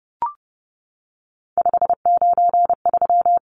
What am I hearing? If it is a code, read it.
Morse code, heard as a keyed tone, about 30 wpm, 700 Hz. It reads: 593
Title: Numbers - 3-Digit 30wpm